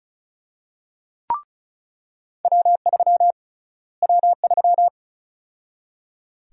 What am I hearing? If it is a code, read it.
Morse code, heard as a keyed tone, about 34 wpm, 700 Hz. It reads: W3 W3
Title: USA Call Sign Prefixes - Repeated 2x - 35wpm